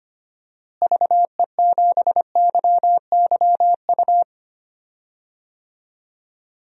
Morse code, heard as a keyed tone, about 25 wpm, 700 Hz. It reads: VE7YYU
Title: DXCC-100 Call Signs - 6 Characters - Repeated 2x - 25wpm